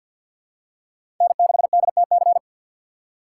Morse code, heard as a keyed tone, about 53 wpm, 700 Hz. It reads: N6DTC
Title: USA Call Signs - 5 Characters - 3x Character Spacing - 50wpm